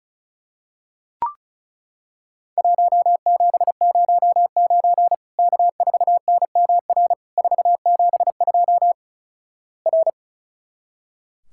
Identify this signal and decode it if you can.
Morse code, heard as a keyed tone, about 34 wpm, 700 Hz. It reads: 1709 K4NMR 472 R
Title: Worked All Europe DX Contest - QTC Exchange - 35wpm